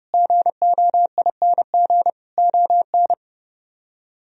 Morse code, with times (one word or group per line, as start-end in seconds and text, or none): 0.1-2.1 GOING
2.4-3.1 ON